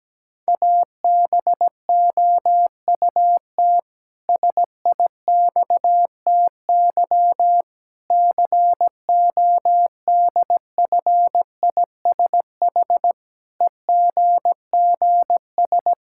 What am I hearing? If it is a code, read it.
Morse code, heard as a keyed tone, about 17 wpm, 700 Hz. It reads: ABOUT SIXTY CODFISH EGGS
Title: Pangram Challenge v2 - 17wpm